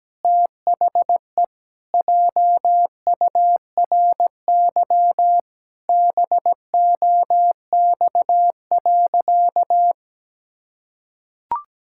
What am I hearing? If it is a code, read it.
Morse code, heard as a keyed tone, about 17 wpm, 700 Hz. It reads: THE JURY BOX.
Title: Pangram Challenge v2 - 17wpm